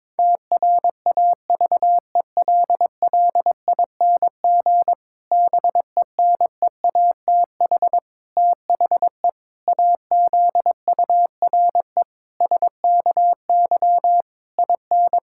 Morse code, heard as a keyed tone, about 21 wpm, 700 Hz. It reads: TRAVELLING BENEATH THE AZURE SKY IN
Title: Pangram Challenge v2 - 22wpm